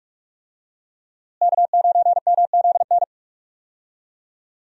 Morse code, heard as a keyed tone, about 43 wpm, 700 Hz. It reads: K9MZN
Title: USA Call Signs - 5 Characters - Repeated 3x - 45wpm